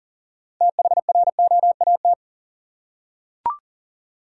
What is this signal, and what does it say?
Morse code, heard as a keyed tone, about 40 wpm, 700 Hz. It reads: THROAT
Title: Six Letter Words - 40wpm